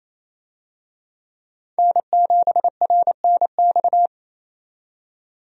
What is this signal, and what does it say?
Morse code, heard as a keyed tone, about 28 wpm, 700 Hz. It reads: N7RNX
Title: USA Call Signs - 5 Characters - 28wpm